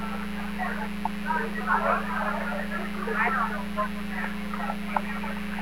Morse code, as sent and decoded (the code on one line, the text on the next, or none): none